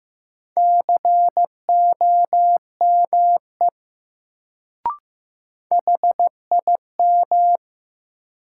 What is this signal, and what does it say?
Morse code, heard as a keyed tone, about 15 wpm, 700 Hz. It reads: COME HIM
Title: Sentences from Top 100 Words - New Word Drill - 15wpm